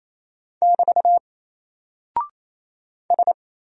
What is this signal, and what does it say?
Morse code, heard as a keyed tone, about 27 wpm, 700 Hz. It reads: <BT> S
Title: Single Character (Letter:Number:Punctuation:Prosign) 28wpm